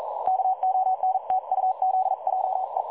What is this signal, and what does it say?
Morse code, heard as a keyed tone, about 41 wpm, 750 Hz. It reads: KONTUZJE
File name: dlazaawansowanych / nagraniazpasma / websdr recording 3565.0kHz